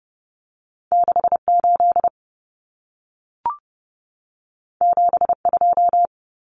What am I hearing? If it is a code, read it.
Morse code, heard as a keyed tone, about 30 wpm, 700 Hz. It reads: 68 72
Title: Common Cut Numbers - 2-Digit 30wpm